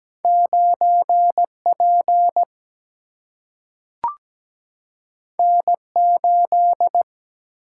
Morse code, heard as a keyed tone, about 17 wpm, 700 Hz. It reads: 9 P N 8
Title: Single-Letter Single-Number 17wpm